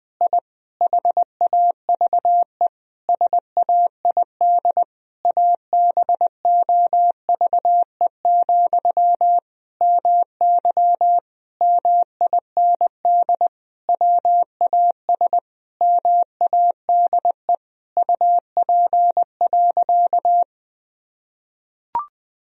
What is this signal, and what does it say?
Morse code, heard as a keyed tone, about 20 wpm, 700 Hz. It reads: I HAVE SAID ABOVE, MY MIND WAS MADE UP.
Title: Sentences from Top 1000 Words - Review Part 2 - 20wpm